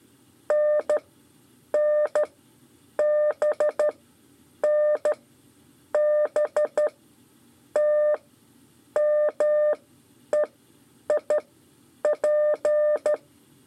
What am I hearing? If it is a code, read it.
Morse code, heard as a keyed tone, about 14 wpm, spaced as for 8 wpm, 600 Hz. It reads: NNBN B T MEIP
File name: 5 LEZIONE RIPASSO 2